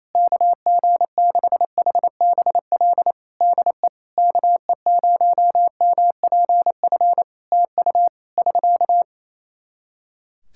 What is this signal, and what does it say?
Morse code, heard as a keyed tone, about 28 wpm, 700 Hz. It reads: KG6HBL DE KE0MPF TU <SK>
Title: Practice QSO - Exchange 4 - Ending - 28wpm